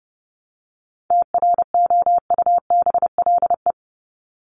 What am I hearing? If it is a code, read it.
Morse code, heard as a keyed tone, about 30 wpm, 700 Hz. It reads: TROUBLE
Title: Top 1001-1250 Words 30wpm